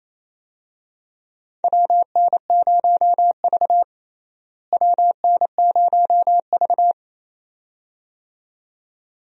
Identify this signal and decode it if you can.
Morse code, heard as a keyed tone, about 28 wpm, 700 Hz. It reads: WN0V WN0V
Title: USA Call Signs - 4 Characters - Repeated 2x - 28wpm